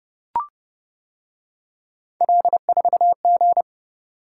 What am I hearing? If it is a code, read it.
Morse code, heard as a keyed tone, about 29 wpm, 700 Hz. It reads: L4G